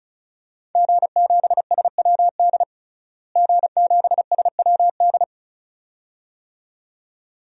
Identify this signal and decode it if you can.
Morse code, heard as a keyed tone, about 34 wpm, 700 Hz. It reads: G7SWD G7SWD
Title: DXCC-100 Call Signs - 5 Characters - Repeated 2x - 35wpm